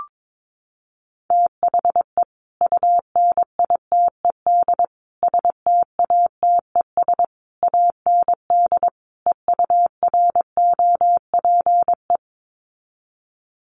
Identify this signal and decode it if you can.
Morse code, heard as a keyed tone, about 22 wpm, 700 Hz. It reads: THE UNITED STATES AND EUROPE
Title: Sets of 5 Words Encore 22wpm